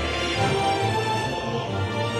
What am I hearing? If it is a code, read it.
There is no code.